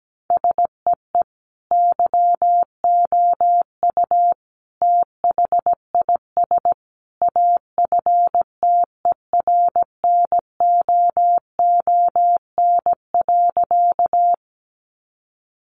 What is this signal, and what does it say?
Morse code, heard as a keyed tone, about 17 wpm, 700 Hz. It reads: SEE YOU THIS AFTERNOON.
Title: Sentences from Top 1500 Words 17wpm